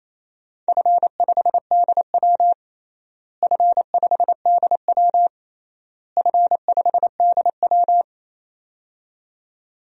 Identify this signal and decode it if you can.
Morse code, heard as a keyed tone, about 27 wpm, 700 Hz. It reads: F5DW F5DW F5DW
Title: DXCC-100 Call Signs - 4 Characters - Repeated 3x - 28wpm